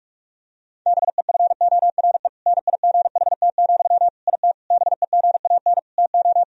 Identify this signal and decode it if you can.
Morse code, heard as a keyed tone, about 41 wpm, 700 Hz. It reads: BEFORE NIGHT, IT BEGAN TO
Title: Sentences from Top 1000 Words - Review Part 1 - 45wpm